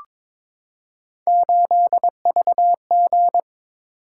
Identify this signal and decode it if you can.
Morse code, heard as a keyed tone, about 22 wpm, 700 Hz. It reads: 8VG
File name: practice / 22wpm / Three Letter-Number Combinations - 22wpm